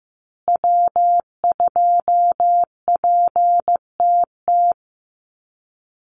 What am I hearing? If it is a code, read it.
Morse code, heard as a keyed tone, about 15 wpm, 700 Hz. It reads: W2PTT
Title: Call Signs Repeated 2x - 15wpm